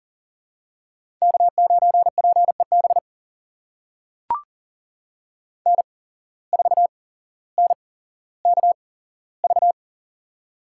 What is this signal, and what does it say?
Morse code, heard as a keyed tone, about 40 wpm, spaced as for 17 wpm, 700 Hz. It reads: K9PEB N4NKV
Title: USA Call Signs - 5 Characters - 8x Character Spacing - 40wpm